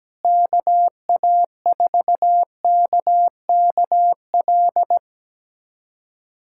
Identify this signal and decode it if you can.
Morse code, heard as a keyed tone, about 17 wpm, 700 Hz. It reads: KA4KKL
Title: USA Callsign DE Callsign - 6 Characters - 17wpm